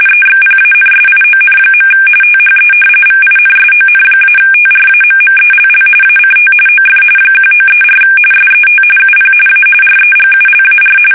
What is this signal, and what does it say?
RTTY, about 45.45 baud. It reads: EXPANSION. THE VINSON-TRAMMELL ACT OF 1934 SET UP A REGULAR
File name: fsk850